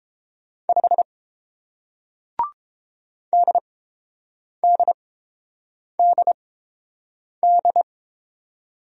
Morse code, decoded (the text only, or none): 5 DDDD